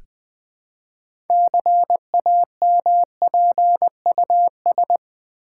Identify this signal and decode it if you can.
Morse code, heard as a keyed tone, about 20 wpm, 700 Hz. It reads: CAMPUS
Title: Build A Word - 3 Letters At A Time - 6-Letter Words - 20wpm